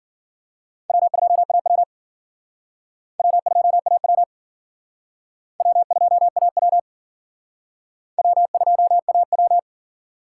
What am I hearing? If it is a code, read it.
Morse code, heard as a keyed tone, about 50 wpm, 700 Hz. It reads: W2AW W2AW W2AW W2AW
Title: Speed Racing - USA 1x2 Call Signs 40wpm